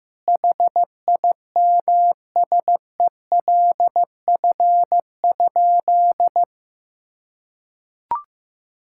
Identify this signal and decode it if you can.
Morse code, heard as a keyed tone, about 15 wpm, 700 Hz. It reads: HIMSELF?
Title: Sentences from Top 1600 Words 15wpm